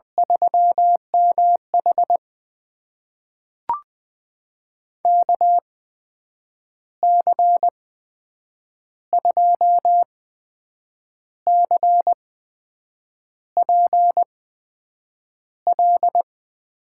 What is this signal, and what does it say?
Morse code, heard as a keyed tone, about 20 wpm, spaced as for 7 wpm, 700 Hz. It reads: 3MH KC2CPL
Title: USA Call Signs - 8x Character Spacing - 20wpm